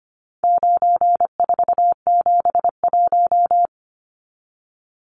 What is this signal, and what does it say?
Morse code, heard as a keyed tone, about 25 wpm, 700 Hz. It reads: 9471
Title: Numbers - 4-Digit 25wpm